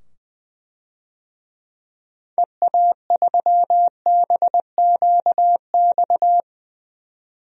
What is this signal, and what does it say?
Morse code, heard as a keyed tone, about 20 wpm, 700 Hz. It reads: EA3BQX
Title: DXCC-100 Call Signs - 6 Characters - 20wpm